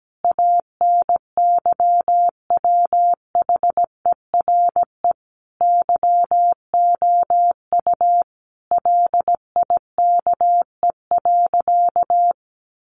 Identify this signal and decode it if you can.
Morse code, heard as a keyed tone, about 17 wpm, 700 Hz. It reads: ANYWHERE YOU LIKE.
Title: Sentences from Top 1400 Words 17wpm